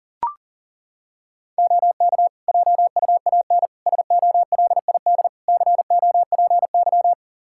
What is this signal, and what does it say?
Morse code, heard as a keyed tone, about 40 wpm, 700 Hz. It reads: OK JUAN SOLID COPY
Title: Practice QSO - Exchange 3 - More Info - 40wpm